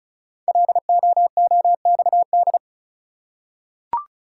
Morse code, heard as a keyed tone, about 36 wpm, 700 Hz. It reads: LOOXD
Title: Code Groups - 5 Letter:Numbers - 35wpm